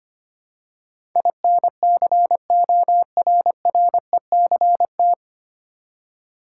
Morse code, heard as a keyed tone, about 25 wpm, 690 Hz. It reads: INCORRECT
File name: practice / 25wpm / 9 Letter Words - First 3 Letters Given - 25wpm